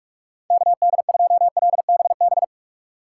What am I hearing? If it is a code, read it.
Morse code, heard as a keyed tone, about 44 wpm, 700 Hz. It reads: KD2LBB